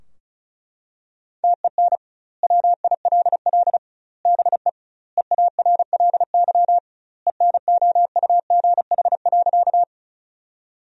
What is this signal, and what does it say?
Morse code, heard as a keyed tone, about 34 wpm, 700 Hz. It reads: TEN WILL BE EARLY ENOUGH.
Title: 2x Word-Spacing - Sentences from Top 400 Words - 35wpm